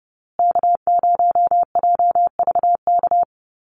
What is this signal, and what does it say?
Morse code, heard as a keyed tone, about 30 wpm, 700 Hz. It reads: K0JVK
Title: USA Call Signs Repeated 3x 30wpm